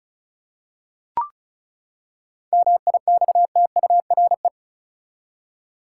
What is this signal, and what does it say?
Morse code, heard as a keyed tone, about 34 wpm, 700 Hz. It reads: MIXTURE
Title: Seven Letter Words - 35wpm